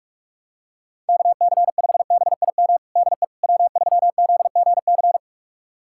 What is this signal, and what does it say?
Morse code, heard as a keyed tone, about 46 wpm, 700 Hz. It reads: KC5BIM DE W3ZGC
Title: USA Callsign DE Callsign 45wpm